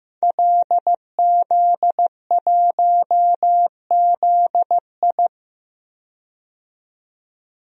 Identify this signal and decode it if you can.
Morse code, heard as a keyed tone, about 15 wpm, 700 Hz. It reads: LZ1ZI
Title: DXCC-100 Call Signs - 5 Characters - 15wpm